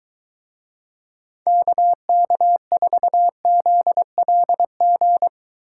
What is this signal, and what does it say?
Morse code, heard as a keyed tone, about 23 wpm, 700 Hz. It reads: KK4ZLG